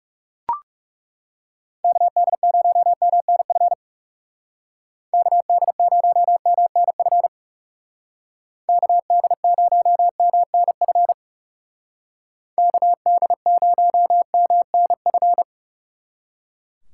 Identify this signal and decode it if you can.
Morse code, heard as a keyed tone, about 37 wpm, 700 Hz. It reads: KD0MNF KD0MNF KD0MNF KD0MNF
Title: Speed Racing - USA Call Signs 30wpm